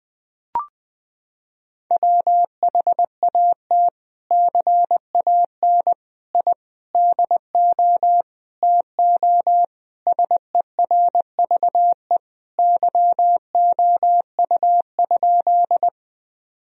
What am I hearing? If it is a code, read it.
Morse code, heard as a keyed tone, about 20 wpm, 700 Hz. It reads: WHAT CAN I DO TO SERVE YOU?
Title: Sentences from Top 1000 Words - Review Part 2 - 20wpm